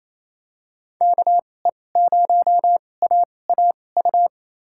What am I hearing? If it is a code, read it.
Morse code, heard as a keyed tone, about 28 wpm, spaced as for 20 wpm, 700 Hz. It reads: KE0AAU
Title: USA Call Signs - 6 Characters - 2x Character Spacing - 28wpm